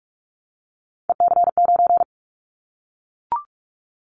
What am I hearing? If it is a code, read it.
Morse code, heard as a keyed tone, about 44 wpm, 700 Hz. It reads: EC9
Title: Callsign Prefixes for Top 100 DXCC - 45wpm